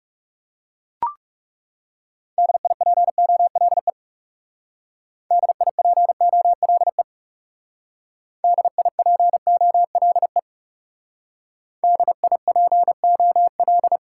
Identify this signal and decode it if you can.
Morse code, heard as a keyed tone, about 38 wpm, 700 Hz. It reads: DIPOLE DIPOLE DIPOLE DIPOL